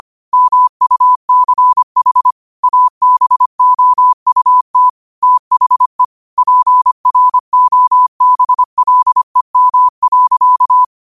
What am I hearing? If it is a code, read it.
Morse code, heard as a keyed tone, about 25 wpm, 1000 Hz. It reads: MUCH ABOUT THE PROBLEM.